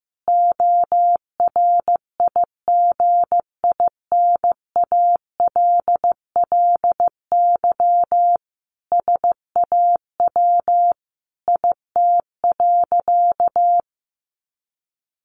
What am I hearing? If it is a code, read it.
Morse code, heard as a keyed tone, about 15 wpm, 700 Hz. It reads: ORIGINALLY SAW IT.
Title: Sentences from Top 1500 Words 15wpm